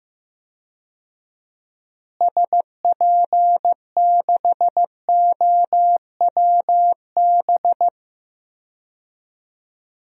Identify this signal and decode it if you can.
Morse code, heard as a keyed tone, about 15 wpm, 700 Hz. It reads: SP6OWB
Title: DXCC-100 Call Signs - 6 Characters - Repeated 2x - 15wpm